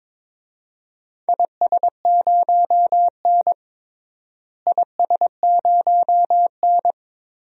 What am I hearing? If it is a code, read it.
Morse code, heard as a keyed tone, about 22 wpm, 700 Hz. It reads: IS0N IS0N
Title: DXCC-100 Call Signs - 4 Characters - Repeated 2x - 22wpm